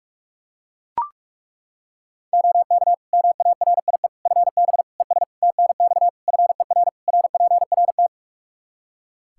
Morse code, heard as a keyed tone, about 41 wpm, 700 Hz. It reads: OK MARIE FB ES TNX FER RPRT
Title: Practice QSO - Exchange 2 - Info - 45wpm